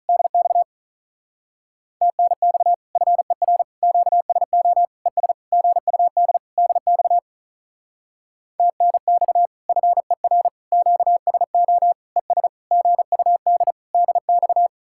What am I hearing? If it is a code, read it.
Morse code, heard as a keyed tone, about 43 wpm, 700 Hz. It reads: DX TNX FER QSO ES GUD DX TNX FER QSO ES GUD DX
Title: Speed-Racing Common QSO Phrases Encore - 35wpm